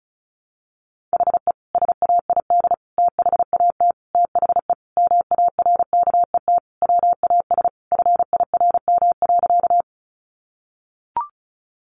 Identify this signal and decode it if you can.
Morse code, heard as a keyed tone, about 34 wpm, 700 Hz. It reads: HE SAID THAT THE MARKET WAS FIRM.
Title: Sentences from Top 1400 Words 35wpm